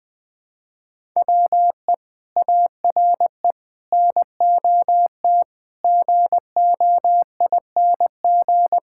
Morse code, heard as a keyed tone, about 20 wpm, 700 Hz. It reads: WE ARE NOT GOING
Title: Sentences with 5-Word Phrases - 20wpm